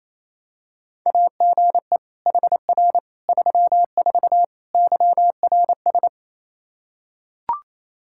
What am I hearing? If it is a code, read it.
Morse code, heard as a keyed tone, about 28 wpm, 700 Hz. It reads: AGE HR 34 YRS
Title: Practice QSO - Exchange 3 - More Info - 28wpm